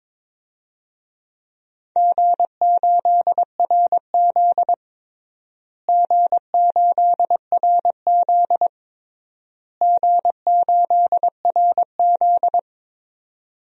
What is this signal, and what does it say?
Morse code, heard as a keyed tone, about 22 wpm, 700 Hz. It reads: G8RZ G8RZ G8RZ
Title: DXCC-100 Call Signs - 4 Characters - Repeated 3x - 22wpm